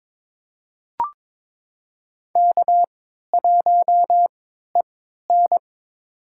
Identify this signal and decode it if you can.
Morse code, heard as a keyed tone, about 22 wpm, spaced as for 13 wpm, 690 Hz. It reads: K1EN
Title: USA Call Signs - 4 Characters - 3x Character Spacing - 22wpm